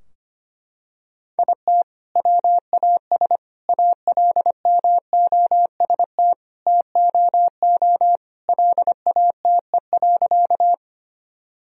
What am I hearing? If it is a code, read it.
Morse code, heard as a keyed tone, about 25 wpm, 700 Hz. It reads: IT WAS ALMOST TOO LATE.
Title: Sentences from Top 1000 Words - Review Part 4 - 25wpm